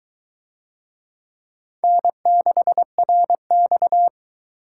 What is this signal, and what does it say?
Morse code, heard as a keyed tone, about 22 wpm, 700 Hz. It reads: N6RX